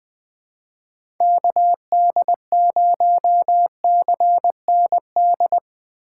Morse code, heard as a keyed tone, about 21 wpm, 700 Hz. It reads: KD0CND
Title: USA Call Signs - 6 Characters - 4x Character Spacing - 20wpm